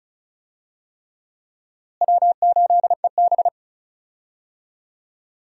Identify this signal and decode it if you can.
Morse code, heard as a keyed tone, about 36 wpm, 700 Hz. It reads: W8EB
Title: USA Call Signs - 4 Characters - Repeated 3x - 35wpm